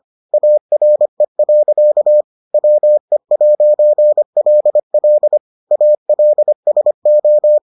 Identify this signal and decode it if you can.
Morse code, heard as a keyed tone, about 25 wpm, 600 Hz. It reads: ARE. WE'LL ALSO